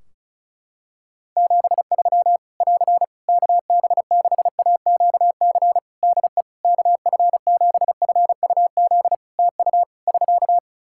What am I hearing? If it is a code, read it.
Morse code, heard as a keyed tone, about 34 wpm, 700 Hz. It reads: 73 <AR> KB6AQC DE KF7FUZ TU <SK>
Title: Practice QSO - Exchange 4 - Ending - 35wpm